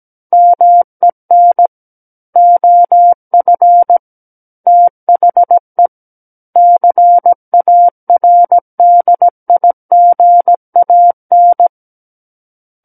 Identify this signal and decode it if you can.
Morse code, heard as a keyed tone, about 17 wpm, 700 Hz. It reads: MEN OF THE CARDIGAN
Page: SKCC Learning Center - Morse Code Practice Files